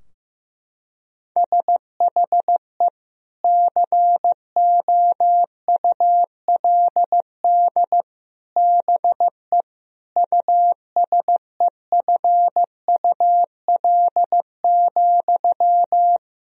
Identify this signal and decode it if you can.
Morse code, heard as a keyed tone, about 15 wpm, 700 Hz. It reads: SHE COULD BE USEFUL,
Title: Sentences from Top 1000 Words - Review Part 4 - 15wpm